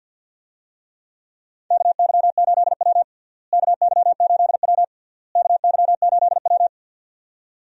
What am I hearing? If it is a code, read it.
Morse code, heard as a keyed tone, about 52 wpm, 700 Hz. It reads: KY8W KY8W KY8W
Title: USA Call Signs - 4 Characters - Repeated 3x - 50wpm